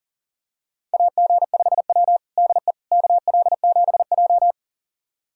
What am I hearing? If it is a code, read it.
Morse code, heard as a keyed tone, about 40 wpm, 690 Hz. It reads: AG5W DE KL7J
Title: USA Callsign DE Callsign - 4 Characters - 40wpm